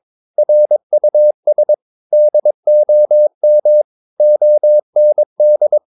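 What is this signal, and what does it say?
Morse code, heard as a keyed tone, about 22 wpm, 600 Hz. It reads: RUS DOM OND
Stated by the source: Morse practice of short Danish words at 22 wpm, from CW med Gnister.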